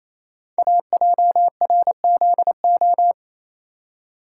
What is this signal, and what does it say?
Morse code, heard as a keyed tone, about 28 wpm, 700 Hz. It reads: AJRZO
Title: Code Groups - 5 Letter:Numbers - 28wpm